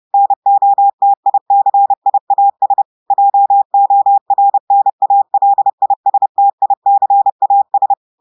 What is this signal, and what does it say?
Morse code, heard as a keyed tone, about 30 wpm, 800 Hz. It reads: NOTICIAS JORNALISTICAS
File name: audios / portugais / portugais texte en clair 2 vitesse 30 mots minutes